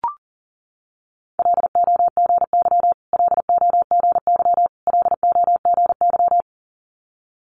Morse code, heard as a keyed tone, about 41 wpm, 700 Hz. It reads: LOGY LOGY LOGY
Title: Multi-letter Phonemes Repeated 3x 40wpm